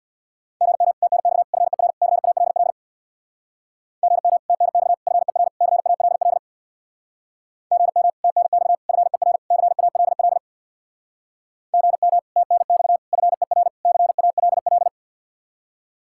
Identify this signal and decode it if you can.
Morse code, heard as a keyed tone, about 64 wpm, 700 Hz. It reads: GM TNX FER CALL GM TNX FER CALL GM TNX FER CALL GM TNX FER CALL